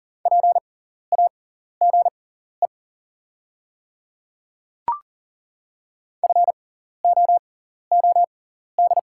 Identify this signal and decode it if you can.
Morse code, heard as a keyed tone, about 40 wpm, spaced as for 14 wpm, 690 Hz. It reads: PAGE FOOD
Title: 6x Character Spacing - Top 300 Words - 40wpm